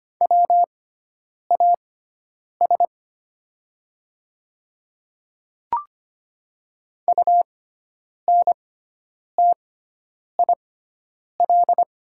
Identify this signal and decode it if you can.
Morse code, heard as a keyed tone, about 25 wpm, spaced as for 9 wpm, 700 Hz. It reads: WAS UNTIL
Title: 6x Character Spacing - Top 300 Words - 25wpm